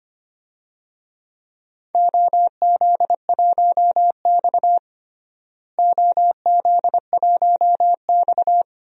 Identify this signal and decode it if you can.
Morse code, heard as a keyed tone, about 25 wpm, 700 Hz. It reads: OZ1X OZ1X
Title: DXCC-100 Call Signs - 4 Characters - Repeated 3x - 25wpm